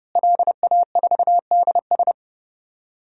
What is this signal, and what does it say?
Morse code, heard as a keyed tone, about 32 wpm, 700 Hz. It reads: LA4DS
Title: DXCC-100 Call Signs - 5 Characters - Repeated 3x - 30wpm